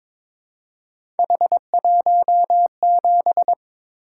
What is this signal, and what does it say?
Morse code, heard as a keyed tone, about 22 wpm, 700 Hz. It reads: H17
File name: practice / 22wpm / Three Letter-Number Combinations - 22wpm